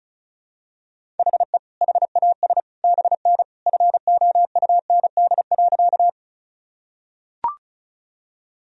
Morse code, heard as a keyed tone, about 36 wpm, 700 Hz. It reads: HE HAS BN FOUND.
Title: Sentences from Top 300 Words and Abbreviations 35wpm